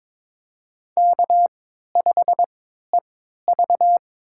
Morse code, heard as a keyed tone, about 22 wpm, spaced as for 13 wpm, 700 Hz. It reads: K5EV
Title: USA Call Signs - 4 Characters - 3x Character Spacing - 22wpm